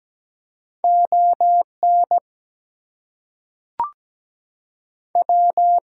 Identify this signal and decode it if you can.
Morse code, heard as a keyed tone, about 17 wpm, 700 Hz. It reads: ON W
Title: Two Letter Words - 17wpm